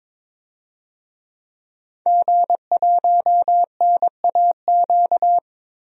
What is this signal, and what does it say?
Morse code, heard as a keyed tone, about 22 wpm, 700 Hz. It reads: G1NAQ
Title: DXCC-100 Call Signs - 5 Characters - 22wpm